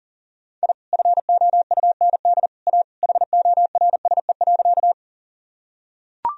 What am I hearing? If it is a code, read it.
Morse code, heard as a keyed tone, about 40 wpm, 700 Hz. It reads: I FOUND A HORSE.
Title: Sentences from Top 1000 Words - Review Part 1 - 40wpm